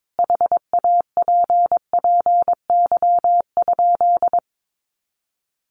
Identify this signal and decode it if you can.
Morse code, heard as a keyed tone, about 22 wpm, 700 Hz. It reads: HAPPY?
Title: Sentences from Top 1000 Words 22wpm